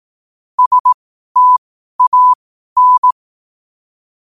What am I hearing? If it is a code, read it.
Morse code, heard as a keyed tone, about 18 wpm, spaced as for 12 wpm, 1000 Hz. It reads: STAN
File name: qotdfiles / 12wpm